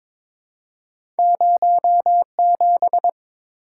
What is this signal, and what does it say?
Morse code, heard as a keyed tone, about 22 wpm, 700 Hz. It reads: 07
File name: practice / 22wpm / Two Letter-Number Combinations - 22wpm